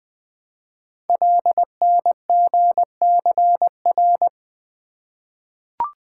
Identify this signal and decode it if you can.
Morse code, heard as a keyed tone, about 20 wpm, 700 Hz. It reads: LNGCR
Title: Code Groups - 5 Letters - 20wpm